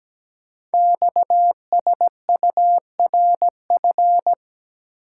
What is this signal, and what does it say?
Morse code, heard as a keyed tone, about 17 wpm, 700 Hz. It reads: XSURF
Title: Code Groups - 5 Letters - 17wpm